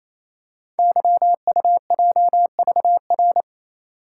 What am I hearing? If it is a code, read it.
Morse code, heard as a keyed tone, about 27 wpm, 700 Hz. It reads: YUJVR
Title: Code Groups - 5 Letters - 28wpm